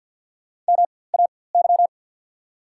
Morse code, heard as a keyed tone, about 51 wpm, spaced as for 23 wpm, 690 Hz. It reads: MAY
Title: 4x Character Spacing - Top 100 Words - 50wpm